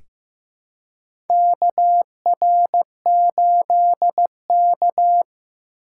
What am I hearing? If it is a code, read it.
Morse code, heard as a keyed tone, about 15 wpm, 700 Hz. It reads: KR8K
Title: USA Call Signs - 4 Characters - 2x Character Spacing - 15wpm